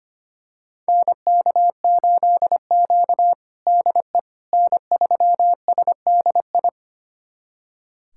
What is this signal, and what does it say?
Morse code, heard as a keyed tone, about 25 wpm, 700 Hz. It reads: NK8Q DE N3SDI
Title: USA Callsign DE Callsign 25wpm